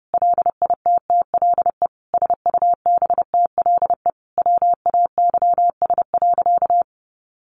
Morse code, heard as a keyed tone, about 30 wpm, 700 Hz. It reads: LITTLE SUBTLE WAYS.
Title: A Sound Of Thunder 30wpm